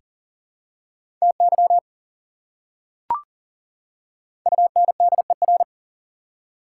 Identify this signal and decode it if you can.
Morse code, heard as a keyed tone, about 37 wpm, 700 Hz. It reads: TY UNDER
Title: Prefixes and Suffixes 40wpm